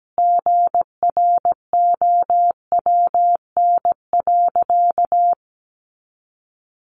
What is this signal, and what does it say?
Morse code, heard as a keyed tone, about 17 wpm, 700 Hz. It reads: GROWN.
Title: Easy Phrases 17wpm